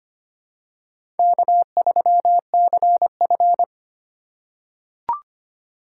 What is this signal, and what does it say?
Morse code, heard as a keyed tone, about 25 wpm, 700 Hz. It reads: K3CF
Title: USA Call Signs - 4 Characters - 4x Character Spacing - 25wpm